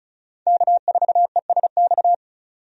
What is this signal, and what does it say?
Morse code, heard as a keyed tone, about 36 wpm, 700 Hz. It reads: K4ESX
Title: USA Call Signs - 5 Characters - 35wpm